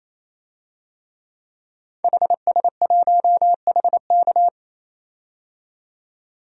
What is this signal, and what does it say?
Morse code, heard as a keyed tone, about 27 wpm, 700 Hz. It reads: HS1HK
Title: DXCC-100 Call Signs - 5 Characters - 28wpm